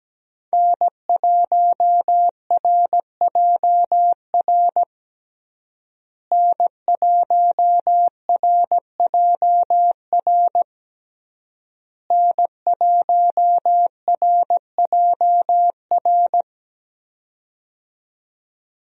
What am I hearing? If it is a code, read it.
Morse code, heard as a keyed tone, about 17 wpm, 700 Hz. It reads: N1RJR N1RJR N1RJR
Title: USA Call Signs - 5 Characters - Repeated 3x - 17wpm